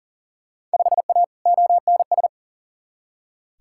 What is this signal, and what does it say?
Morse code, heard as a keyed tone, about 40 wpm, 700 Hz. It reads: 5A ONS
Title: Field Day Exchanges Encore - 40wpm